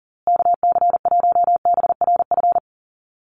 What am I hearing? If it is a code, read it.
Morse code, heard as a keyed tone, about 40 wpm, 700 Hz. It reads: KC1BRF
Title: USA Call Signs Repeated 2x 40wpm